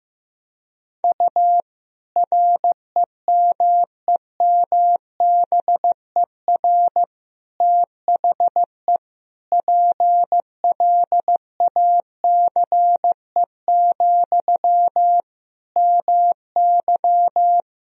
Morse code, heard as a keyed tone, about 16 wpm, 700 Hz. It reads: U REMEMBER THE PLACE, MY
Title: Sentences from Top 400 Words and Abbreviations 15wpm